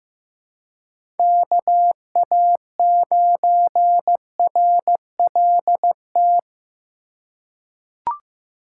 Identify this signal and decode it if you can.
Morse code, heard as a keyed tone, about 15 wpm, 700 Hz. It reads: KA9RLT
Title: USA Call Signs - 6 Characters - 4x Character Spacing - 15wpm